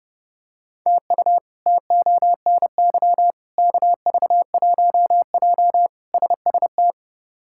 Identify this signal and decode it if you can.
Morse code, heard as a keyed tone, about 30 wpm, 700 Hz. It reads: TU TONY KV1J SST
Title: K1USN SST QSO Practice - 30wpm